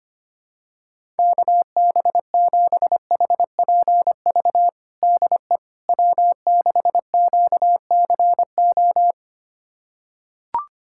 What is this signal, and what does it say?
Morse code, heard as a keyed tone, about 25 wpm, 700 Hz. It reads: KB7HPV DE W6QCO
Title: USA Callsign DE Callsign 25wpm